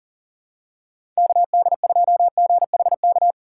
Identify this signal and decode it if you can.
Morse code, heard as a keyed tone, about 40 wpm, 700 Hz. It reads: KD2GHK
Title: USA Call Signs - 6 Characters - 2x Character Spacing - 40wpm